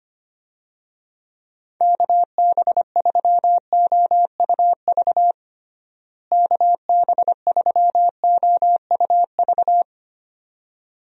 Morse code, heard as a keyed tone, about 25 wpm, 700 Hz. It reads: KB3OUV KB3OUV
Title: USA Call Signs - 6 Characters - Repeated 2x - 25wpm